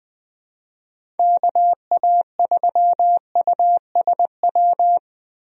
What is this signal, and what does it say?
Morse code, heard as a keyed tone, about 20 wpm, 700 Hz. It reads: KA3USW
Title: USA Call Signs - 6 Characters - 20wpm